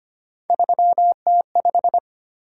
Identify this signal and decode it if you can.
Morse code, heard as a keyed tone, about 29 wpm, 700 Hz. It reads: 3T5
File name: Speed-Racing - Three Letter-Number Combinations - 25wpm